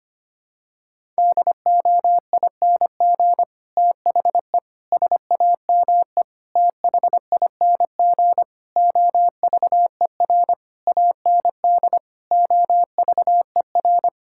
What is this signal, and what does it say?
Morse code, heard as a keyed tone, about 25 wpm, 700 Hz. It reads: DOING THE SAME THING OVER AND OVER
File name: practice / 25wpm / Sentences with 9-Word Phrases - 25wpm